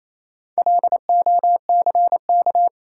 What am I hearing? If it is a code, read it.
Morse code, heard as a keyed tone, about 28 wpm, 700 Hz. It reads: LOCK
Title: Sentences from Top 1300 Words - New Word Drill Alt - 28wpm